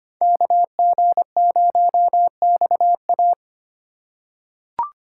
Morse code, heard as a keyed tone, about 25 wpm, 700 Hz. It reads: KG0XA
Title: USA Call Signs - 5 Characters - 4x Character Spacing - 25wpm